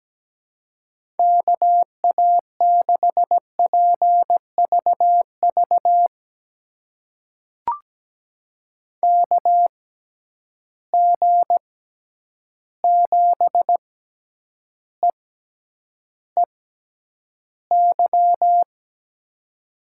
Morse code, heard as a keyed tone, about 17 wpm, spaced as for 9 wpm, 700 Hz. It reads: KA6PVV KG7EEY
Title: USA Call Signs - 6 Characters - 6x Character Spacing - 17wpm